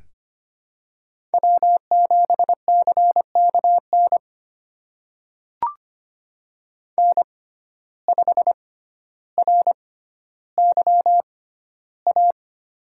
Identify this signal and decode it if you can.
Morse code, heard as a keyed tone, about 25 wpm, spaced as for 13 wpm, 700 Hz. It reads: W7CKN N5RYA
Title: USA Call Signs - 5 Characters - 6x Character Spacing - 25wpm